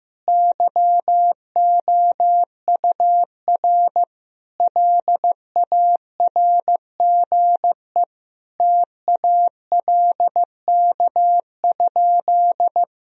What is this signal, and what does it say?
Morse code, heard as a keyed tone, about 15 wpm, 700 Hz. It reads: YOUR LARGE TALK?
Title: Sentences from Top 1000 Words - Review Part 1 - 15wpm